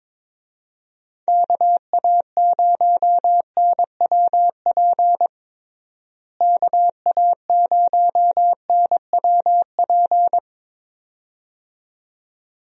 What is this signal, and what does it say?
Morse code, heard as a keyed tone, about 22 wpm, 700 Hz. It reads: KA0NWP KA0NWP
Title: USA Call Signs - 6 Characters - Repeated 2x - 22wpm